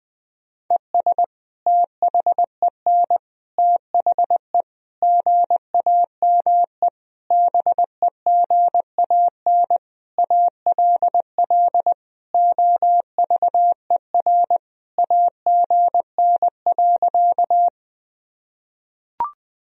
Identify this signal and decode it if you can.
Morse code, heard as a keyed tone, about 20 wpm, 700 Hz. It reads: ES THEN THE GAME BEGAN ALL OVER AGN.
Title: Sentences from Top 500 Words and Abbreviations 20wpm